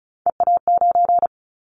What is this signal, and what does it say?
Morse code, heard as a keyed tone, about 34 wpm, 700 Hz. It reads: EA9
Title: Callsign Prefixes for Top 100 DXCC - 35wpm